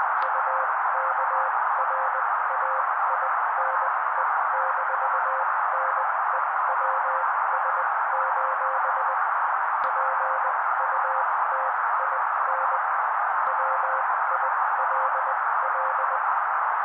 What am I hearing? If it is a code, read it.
Morse code, heard as a keyed tone, about 20 wpm, spaced as for 16 wpm, 530 Hz. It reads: UKRAINE-NEWS: PUTIN WILL
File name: daily / 16wpm noise